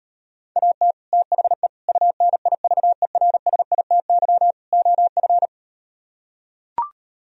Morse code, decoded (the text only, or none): AT THE UNIVERSITY OF